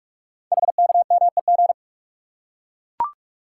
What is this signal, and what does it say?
Morse code, heard as a keyed tone, about 44 wpm, 700 Hz. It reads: HKMEG
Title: Code Groups - 5 Letters - 45wpm